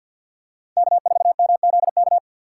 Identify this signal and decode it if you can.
Morse code, heard as a keyed tone, about 54 wpm, 700 Hz. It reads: K4MZK
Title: USA Call Signs - 5 Characters - 8x Character Spacing - 50wpm